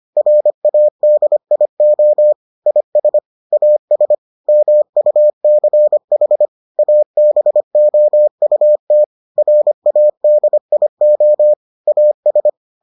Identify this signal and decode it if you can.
Morse code, heard as a keyed tone, about 25 wpm, 600 Hz. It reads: RADIO IS AS MUCH ABOUT RADIO AS